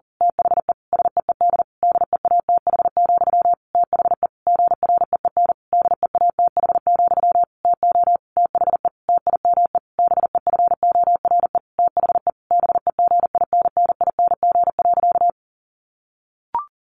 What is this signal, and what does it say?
Morse code, heard as a keyed tone, about 40 wpm, 700 Hz. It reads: THE SEED DEATH, THE GREEN DEATH, TO THE TIME BEFORE THE BEGINNING.
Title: A Sound Of Thunder 40wpm